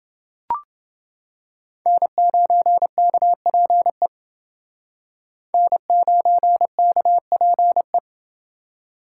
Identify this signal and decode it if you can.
Morse code, heard as a keyed tone, about 28 wpm, 700 Hz. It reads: N9KPE N9KPE